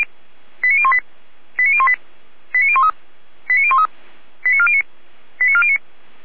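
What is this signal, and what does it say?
Five-tone selective call (ZVEI), heard as quick runs of five tones, tones of about 70 ms. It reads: 89018 89018 89013 89013 89409 89409
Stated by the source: ZVEI five-tone selective call received by SDR.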